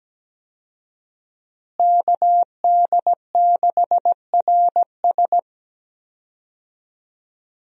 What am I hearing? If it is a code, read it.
Morse code, heard as a keyed tone, about 17 wpm, 700 Hz. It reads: KD6RS
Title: USA Call Signs - 5 Characters - Repeated 3x - 17wpm